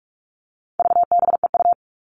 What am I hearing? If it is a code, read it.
Morse code, heard as a keyed tone, about 44 wpm, 700 Hz. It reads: VBEU
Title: All Cut Numbers - 4-Digit 45wpm